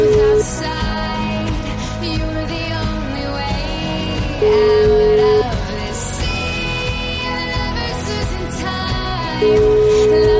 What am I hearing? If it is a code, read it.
Ringback tone (Europe, 425 Hz), with noise.